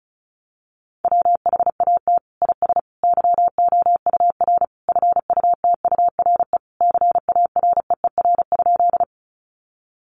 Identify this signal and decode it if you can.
Morse code, heard as a keyed tone, about 34 wpm, 700 Hz. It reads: WHAT IS YOUR FUTURE CAREER?
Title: Sentences from Top 1000 Words 35wpm